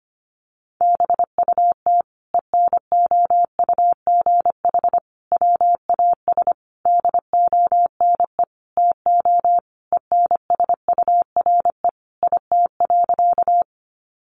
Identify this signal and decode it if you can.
Morse code, heard as a keyed tone, about 25 wpm, 700 Hz. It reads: BUT ENOUGH WAS DONE TO ENSURE IT.
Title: Sentences from Top 1100 Words 25wpm